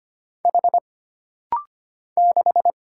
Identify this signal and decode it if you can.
Morse code, heard as a keyed tone, about 25 wpm, 700 Hz. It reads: H 6
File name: Single Character (Letter:Number:Punctuation:Prosign) - Rapid-Fire - 25wpm